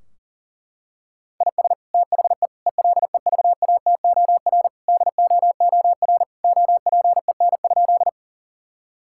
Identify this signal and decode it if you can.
Morse code, heard as a keyed tone, about 40 wpm, 700 Hz. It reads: IS THE ELEVATOR DOOR OPEN?
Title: Sentences from Top 1600 Words 40wpm